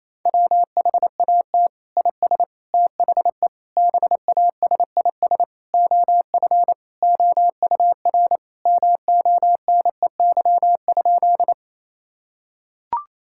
Morse code, heard as a keyed tone, about 28 wpm, 700 Hz. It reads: WHAT IS THE BASIS OF OUR MONEY?
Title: Sentences from Top 1000 Words - Review Part 5 - 28wpm